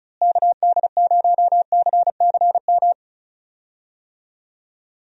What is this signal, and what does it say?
Morse code, heard as a keyed tone, about 34 wpm, 700 Hz. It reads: KD0CCM
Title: USA Call Signs - 6 Characters - Repeated 2x - 35wpm